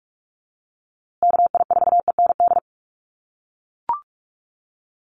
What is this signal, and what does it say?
Morse code, heard as a keyed tone, about 44 wpm, 700 Hz. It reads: KI4END
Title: USA Call Signs Repeated 3x 45wpm